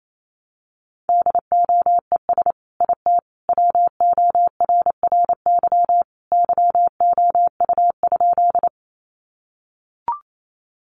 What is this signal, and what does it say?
Morse code, heard as a keyed tone, about 28 wpm, 700 Hz. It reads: DOES IT WORRY YOU?
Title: Sentences from Top 1100 Words 28wpm